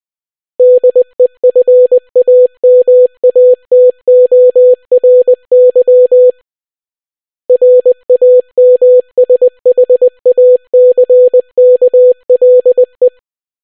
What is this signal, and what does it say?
Morse code, heard as a keyed tone, about 20 wpm, 500 Hz. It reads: DEFAMATORY RAMSHACKLE